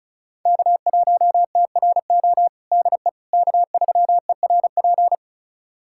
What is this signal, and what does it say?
Morse code, heard as a keyed tone, about 36 wpm, 700 Hz. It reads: K1TRO DE K3ERP
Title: USA Callsign DE Callsign 35wpm